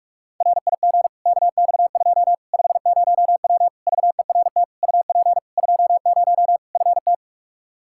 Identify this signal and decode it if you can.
Morse code, heard as a keyed tone, about 46 wpm, 700 Hz. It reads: RIG KX2 50W VERT UP 20 FT
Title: Practice QSO - Exchange 2 - Info - 45wpm